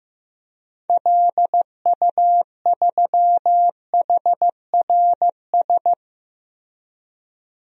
Morse code, heard as a keyed tone, about 15 wpm, 700 Hz. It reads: LU3HRS
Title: DXCC-100 Call Signs - 6 Characters - 15wpm